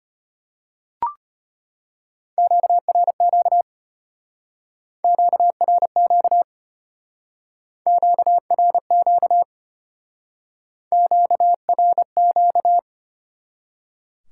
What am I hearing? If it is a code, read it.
Morse code, heard as a keyed tone, about 32 wpm, 700 Hz. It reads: QRQ QRQ QRQ QRQ